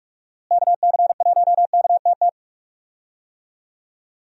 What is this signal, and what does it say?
Morse code, heard as a keyed tone, about 44 wpm, 700 Hz. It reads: KC1KTT
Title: USA Call Signs - 6 Characters - Repeated 3x - 45wpm